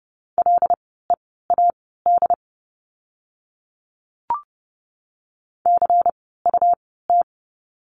Morse code, heard as a keyed tone, about 30 wpm, spaced as for 17 wpm, 700 Hz. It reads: LEAD CUT
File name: Lesson 17f - Letter u - Words 30wpm